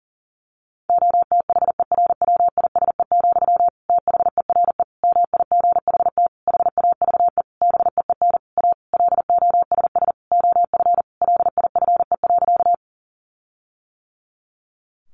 Morse code, heard as a keyed tone, about 40 wpm, 700 Hz. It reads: OTHERWISE, THERE MIGHT HAVE BEEN A LOSS OF LIFE.
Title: Sentences from Top 1400 Words 40wpm